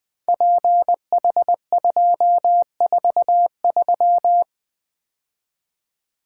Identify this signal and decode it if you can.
Morse code, heard as a keyed tone, about 20 wpm, 700 Hz. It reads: PH243
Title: Code Groups - 5 Letter:Numbers - 20wpm